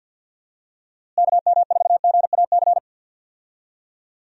Morse code, heard as a keyed tone, about 50 wpm, 700 Hz. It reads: KM4GAC
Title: USA Call Signs - 6 Characters - Repeated 3x - 50wpm